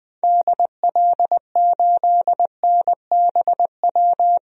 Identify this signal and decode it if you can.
Morse code, heard as a keyed tone, about 20 wpm, 700 Hz. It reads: DL8NBW
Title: DXCC-100 Call Signs - 6 Characters - 20wpm